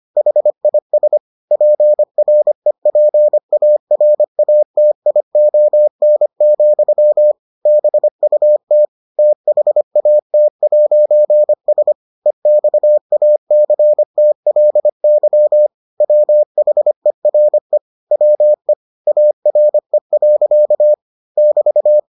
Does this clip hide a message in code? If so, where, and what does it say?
Yes: Morse code from 0.2 to 22.0 s, heard as a keyed tone, about 25 wpm, 600 Hz. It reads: HIS PREPARATION, BUT THAT'S EXACTLY WHERE WE ARE. <BT>